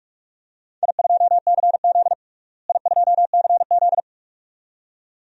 Morse code, heard as a keyed tone, about 46 wpm, 700 Hz. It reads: I2CZ I2CZ
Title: DXCC-100 Call Signs - 4 Characters - Repeated 2x - 45wpm